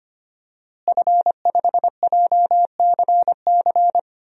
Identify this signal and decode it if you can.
Morse code, heard as a keyed tone, about 25 wpm, 700 Hz. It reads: F5JCC
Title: DXCC-100 Call Signs - 5 Characters - Repeated 3x - 25wpm